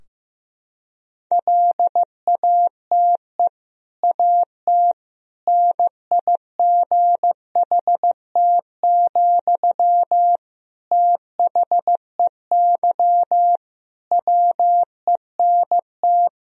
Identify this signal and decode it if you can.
Morse code, heard as a keyed tone, about 15 wpm, 700 Hz. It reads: LATE AT NIGHT, THEY WENT
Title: Sentences from Top 1000 Words - Review Part 3 - 15wpm